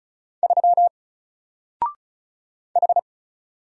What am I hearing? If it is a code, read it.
Morse code, heard as a keyed tone, about 36 wpm, 700 Hz. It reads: 3 H
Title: Single Character (Letter:Number:Punctuation:Prosign) 35wpm